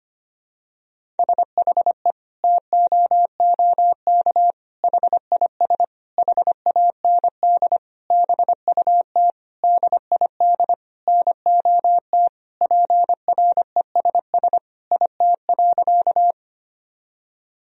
Morse code, heard as a keyed tone, about 25 wpm, 700 Hz. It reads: SHE TOOK HIS HAND BUT DID NOT PRESS IT.
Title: Sentences from Top 1000 Words - Review Part 1 - 25wpm